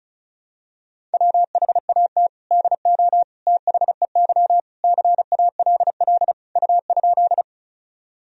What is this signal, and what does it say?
Morse code, heard as a keyed tone, about 36 wpm, 700 Hz. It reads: WHAT DO THEY CALL U?
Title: Sentences from Top 100 Words and Abbreviations 35wpm